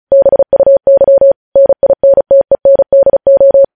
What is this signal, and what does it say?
Morse code, heard as a keyed tone, about 36 wpm, 550 Hz. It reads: BUY NINTENDO
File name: podcast / News headlines-1x-35x35